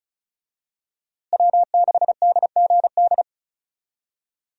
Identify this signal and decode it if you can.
Morse code, heard as a keyed tone, about 36 wpm, 700 Hz. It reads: W6DGD
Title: USA Call Signs - 5 Characters - Repeated 3x - 35wpm